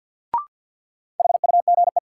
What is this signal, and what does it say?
Morse code, heard as a keyed tone, about 47 wpm, 700 Hz. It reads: HUGE
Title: Sentences from Top 1100 Words - New Word Drill Alt - 50wpm